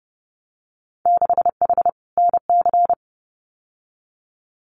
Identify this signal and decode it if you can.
Morse code, heard as a keyed tone, about 30 wpm, 700 Hz. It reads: 6H NC
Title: Winter Field Day Exchanges Encore 30wpm